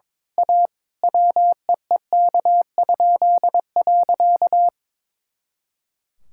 Morse code, heard as a keyed tone, about 22 wpm, 700 Hz. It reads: A WEEK?.
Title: Pangram Challenge v2 - 22wpm